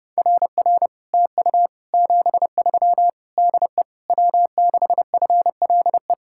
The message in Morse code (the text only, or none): RR TU 73 DE W6FLE